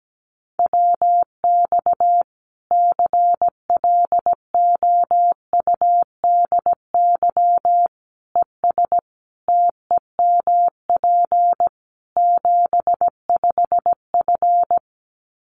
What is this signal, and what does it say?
Morse code, heard as a keyed tone, about 17 wpm, 700 Hz. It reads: WX CLOUDY ES TEMP 75F
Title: Common QSO Phrases 17wpm